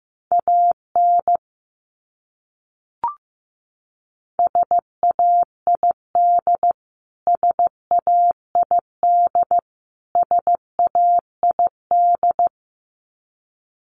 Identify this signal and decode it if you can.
Morse code, heard as a keyed tone, about 15 wpm, 700 Hz. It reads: AN SAID SAID SAID
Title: Top 500 Words Repeated 15wpm